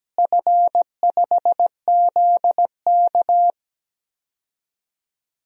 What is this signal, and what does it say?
Morse code, heard as a keyed tone, about 17 wpm, 700 Hz. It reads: F5ZK
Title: DXCC-100 Call Signs - 4 Characters - Repeated 2x - 17wpm